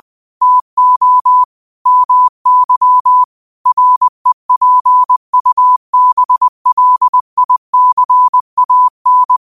Morse code, heard as a keyed tone, about 20 wpm, 1000 Hz. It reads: TO MY REPUBLICAN